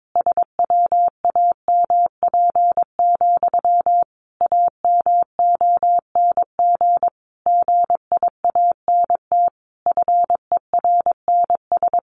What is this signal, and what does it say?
Morse code, heard as a keyed tone, about 22 wpm, 700 Hz. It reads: SWAMP, AMONG GIANT FERNS
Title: A Sound Of Thunder 22wpm